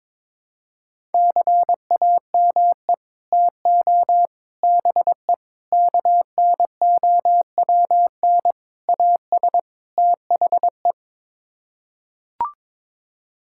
Morse code, heard as a keyed tone, about 21 wpm, 700 Hz. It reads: CAME TO BE KNOWN AS THE
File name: practice / 22wpm / Sets of 6 Words - 22wpm